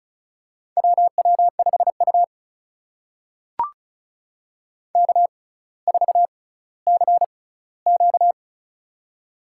Morse code, heard as a keyed tone, about 36 wpm, spaced as for 18 wpm, 700 Hz. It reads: WW5U K4CQ
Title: USA Call Signs - 4 Characters - 6x Character Spacing - 35wpm